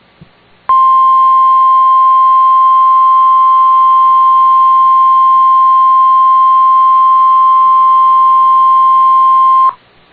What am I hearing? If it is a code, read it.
Emergency Alert System (SAME) tones (the 1050 Hz attention tone). It is heard from 0.7 to 9.8 s.